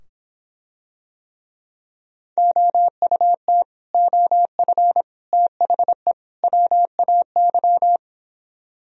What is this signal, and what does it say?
Morse code, heard as a keyed tone, about 26 wpm, 700 Hz. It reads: OUT OF THE WAY